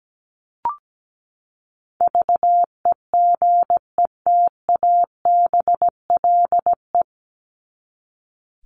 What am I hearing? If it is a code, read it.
Morse code, heard as a keyed tone, about 17 wpm, 700 Hz. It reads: VEGETABLE
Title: Top 1251-1500 Words 17wpm